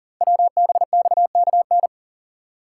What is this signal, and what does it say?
Morse code, heard as a keyed tone, about 40 wpm, 700 Hz. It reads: WBXKN
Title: Code Groups - 5 Letters - 40wpm